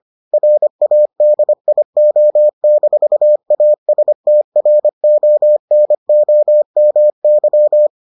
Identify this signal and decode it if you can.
Morse code, heard as a keyed tone, about 25 wpm, 600 Hz. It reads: RADIO-ASTRONOMY